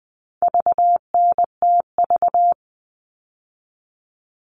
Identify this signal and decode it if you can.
Morse code, heard as a keyed tone, about 20 wpm, 700 Hz. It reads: VNTV
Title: All Cut Numbers - 4-Digit 20wpm